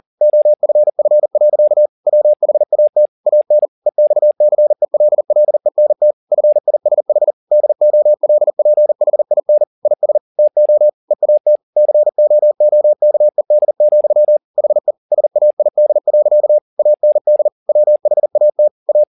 Morse code, heard as a keyed tone, about 40 wpm, 600 Hz. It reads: OFF. WHAT AN EXCELLENT FISH DOLPHIN IS TO EAT COOKED, HE SAID. AND WHAT A
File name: the old man and the sea 2 40wpm